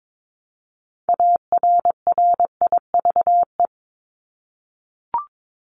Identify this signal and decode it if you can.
Morse code, heard as a keyed tone, about 22 wpm, 700 Hz. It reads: ARRIVE
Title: ED Suffix - Top 100 Words with Base Words 22wpm